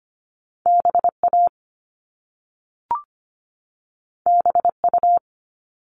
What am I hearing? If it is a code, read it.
Morse code, heard as a keyed tone, about 25 wpm, 700 Hz. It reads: BA BU
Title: All Cut Numbers - 2-Digit 25wpm